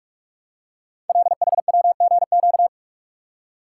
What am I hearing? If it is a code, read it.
Morse code, heard as a keyed tone, about 46 wpm, 700 Hz. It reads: LHWGQ
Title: Code Groups - 5 Letters - 45wpm